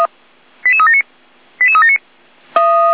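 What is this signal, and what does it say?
Five-tone selective call (ZVEI), heard as quick runs of five tones, tones of about 70 ms. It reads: 88379 88379
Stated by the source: Five-tone ZVEI selective call received on an SDR, 455.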